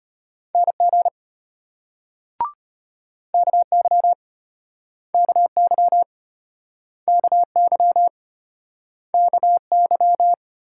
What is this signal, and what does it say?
Morse code, heard as a keyed tone, about 35 wpm, 700 Hz. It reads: NG KY KY KY KY